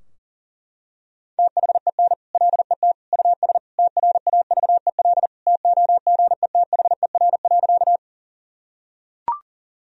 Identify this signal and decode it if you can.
Morse code, heard as a keyed tone, about 40 wpm, 700 Hz. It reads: THEN LET US TRAVEL TOGETHER.
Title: Sentences from Top 1000 Words - Review Part 2 - 40wpm